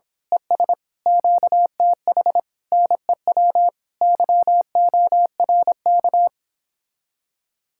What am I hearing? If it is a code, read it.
Morse code, heard as a keyed tone, about 26 wpm, 700 Hz. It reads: ES QTH NEW YORK
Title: Speed-Racing Common QSO Phrases - 22wpm